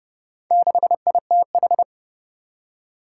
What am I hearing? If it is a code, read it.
Morse code, heard as a keyed tone, about 30 wpm, 700 Hz. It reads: 6ITH
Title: Four Challenging Letter-Number Combinations 30wpm